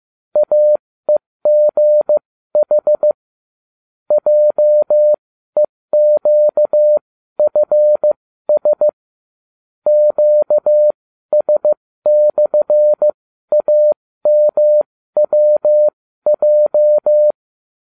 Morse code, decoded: AEGH JEQFS QS/AMWJ

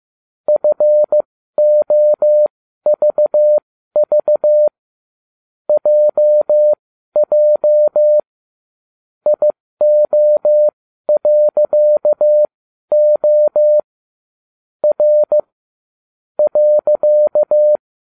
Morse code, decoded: FOVV JJ IO.O R .

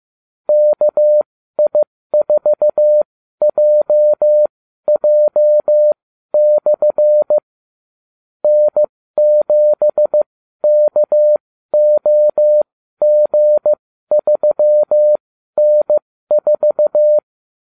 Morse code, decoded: KI4JJ/ N7KOG3N4

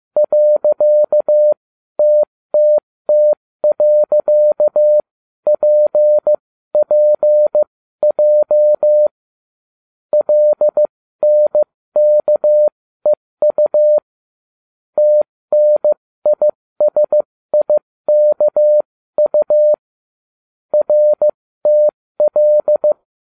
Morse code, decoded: .TTT.PPJ LNKEU TNISIKU RTL

